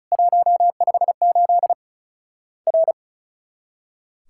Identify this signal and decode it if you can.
Morse code, heard as a keyed tone, about 36 wpm, 700 Hz. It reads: 158 R
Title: Worked All Europe DX Contest - QTC Exchange - 35wpm